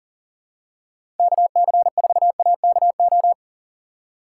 Morse code, heard as a keyed tone, about 40 wpm, 700 Hz. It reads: KC4AKO
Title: USA Call Signs - 6 Characters - 2x Character Spacing - 40wpm